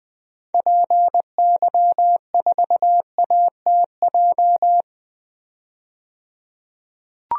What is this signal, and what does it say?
Morse code, heard as a keyed tone, about 20 wpm, 700 Hz. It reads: PY4ATJ
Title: DXCC-100 Call Signs - 6 Characters - 20wpm